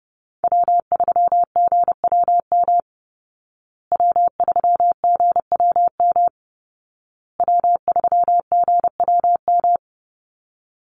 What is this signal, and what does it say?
Morse code, heard as a keyed tone, about 30 wpm, 700 Hz. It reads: W3GWM W3GWM W3GWM
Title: Call Signs Repeated 2x - 30wpm